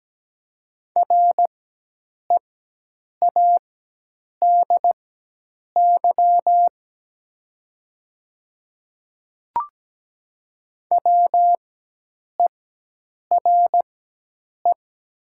Morse code, heard as a keyed tone, about 17 wpm, spaced as for 8 wpm, 700 Hz. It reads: READY WERE
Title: 4x Character Spacing - Top 500 Words - 17wpm